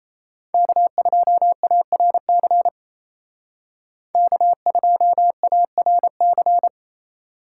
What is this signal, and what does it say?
Morse code, heard as a keyed tone, about 34 wpm, 700 Hz. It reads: K2ARC K2ARC
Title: Speed Racing - USA Call Signs 28wpm